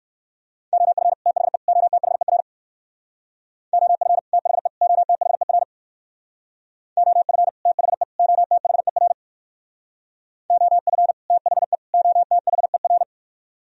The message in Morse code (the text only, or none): OF THE OTHER OF THE OTHER OF THE OTHER OF THE OTHER